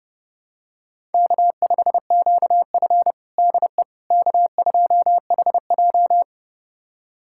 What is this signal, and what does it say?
Morse code, heard as a keyed tone, about 30 wpm, 700 Hz. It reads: K5QF DE K2HJ
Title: USA Callsign DE Callsign - 4 Characters - 30wpm